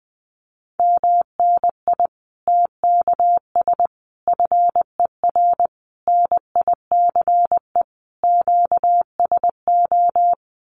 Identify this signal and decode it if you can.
Morse code, heard as a keyed tone, about 20 wpm, 700 Hz. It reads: MNI TKS FER NICE QSO
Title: Common QSO Phrases 20wpm